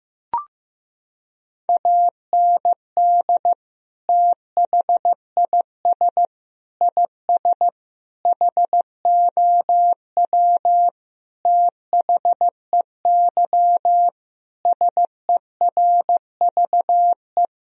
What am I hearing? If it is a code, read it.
Morse code, heard as a keyed tone, about 15 wpm, 700 Hz. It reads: AND THIS IS HOW THEY SERVE
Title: Sentences from Top 1000 Words - Review Part 2 - 15wpm